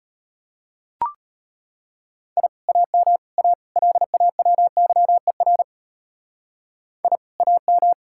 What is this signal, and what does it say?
Morse code, heard as a keyed tone, about 31 wpm, 700 Hz. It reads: I AM A LAWYER I AM
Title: Speed-Racing Common QSO Phrases Encore - 25wpm